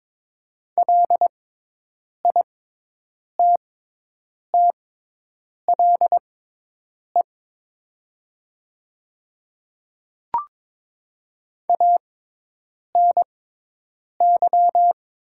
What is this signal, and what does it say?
Morse code, heard as a keyed tone, about 22 wpm, spaced as for 8 wpm, 700 Hz. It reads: LITTLE ANY
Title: 6x Character Spacing - Top 200 Words - 22wpm